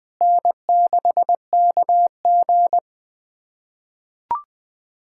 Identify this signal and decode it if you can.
Morse code, heard as a keyed tone, about 20 wpm, 700 Hz. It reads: N6KG
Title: USA Callsign DE Callsign - 4 Characters - 20wpm